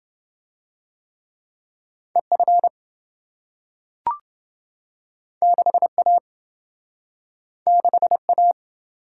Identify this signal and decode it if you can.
Morse code, heard as a keyed tone, about 27 wpm, 700 Hz. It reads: EF 6A 6A